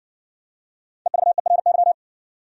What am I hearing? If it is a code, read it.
Morse code, heard as a keyed tone, about 59 wpm, 700 Hz. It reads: EVERY